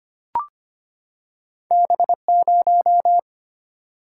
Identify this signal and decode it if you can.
Morse code, heard as a keyed tone, about 25 wpm, 690 Hz. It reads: B0
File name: Two Letter-Number Combinations - 25wpm